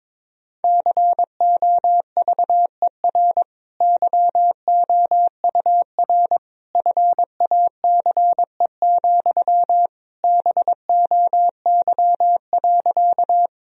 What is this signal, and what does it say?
Morse code, heard as a keyed tone, about 22 wpm, 700 Hz. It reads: COVER YOUR FACE, BOY.
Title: Sentences from Top 1000 Words - Review Part 1 - 22wpm